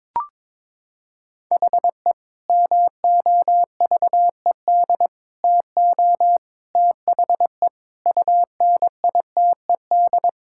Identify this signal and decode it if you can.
Morse code, heard as a keyed tone, about 22 wpm, 700 Hz. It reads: HE MOVED TO THE UNITED
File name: practice / 22wpm / Sentences with 3-Word Phrases - 22wpm